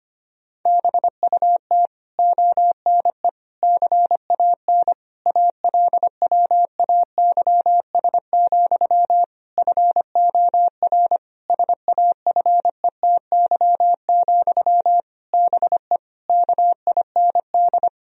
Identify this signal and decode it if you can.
Morse code, heard as a keyed tone, about 25 wpm, 700 Hz. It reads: BUT ONE CAN ALWAYS, FOR SAFETY, BE KIND
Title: Sentences from Top 1000 Words - Review Part 4 - 25wpm